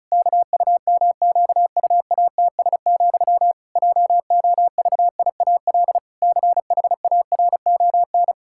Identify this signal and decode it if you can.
Morse code, heard as a keyed tone, about 36 wpm, 700 Hz. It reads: KUMQUATS, JOVIAL CHARON
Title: Pangram Challenge v2 - 35wpm